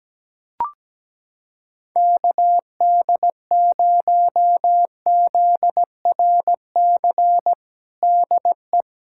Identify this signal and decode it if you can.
Morse code, heard as a keyed tone, about 17 wpm, 700 Hz. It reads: KD0ZRC DE
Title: Practice QSO - Exchange 4 - Ending - 17wpm